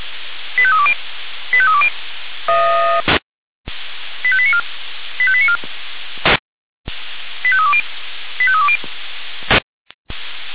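Five-tone selective call (ZVEI), heard as quick runs of five tones, tones of about 70 ms. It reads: 85320 85320 85894 85894 85320 85320
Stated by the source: ZVEI five-tone call received via SDR.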